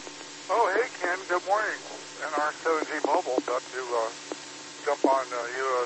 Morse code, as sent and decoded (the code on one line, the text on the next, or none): none